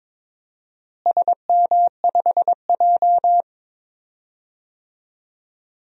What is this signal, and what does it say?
Morse code, heard as a keyed tone, about 22 wpm, 700 Hz. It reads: SM5J
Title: DXCC-100 Call Signs - 4 Characters - Repeated 3x - 22wpm